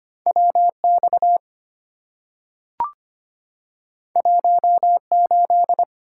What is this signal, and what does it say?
Morse code, heard as a keyed tone, about 25 wpm, 700 Hz. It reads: WX 18
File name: Two Letter-Number Combinations - 25wpm